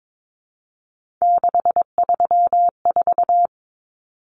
Morse code, heard as a keyed tone, about 22 wpm, 700 Hz. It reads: 634
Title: Common Cut Numbers - 3-Digit 22wpm